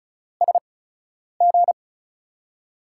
Morse code, heard as a keyed tone, about 34 wpm, spaced as for 10 wpm, 700 Hz. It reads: SG
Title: USA Call Signs - 6 Characters - 8x Character Spacing - 35wpm